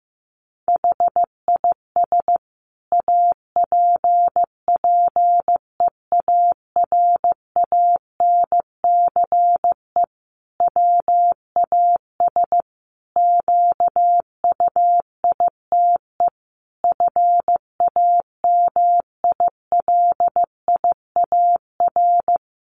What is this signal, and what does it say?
Morse code, heard as a keyed tone, about 15 wpm, 700 Hz. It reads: HIS APPEARANCE WAS QUITE FAMILIAR
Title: Sentences from Top 1100 Words 15wpm